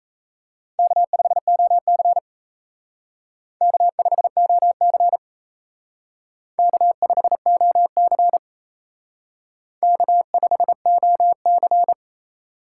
Morse code, heard as a keyed tone, about 35 wpm, 700 Hz. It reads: K5OC K5OC K5OC K5OC